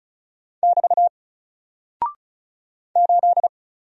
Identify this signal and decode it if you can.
Morse code, heard as a keyed tone, about 35 wpm, 700 Hz. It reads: <BT> 8
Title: Single Character (Letter:Number:Punctuation:Prosign) 35wpm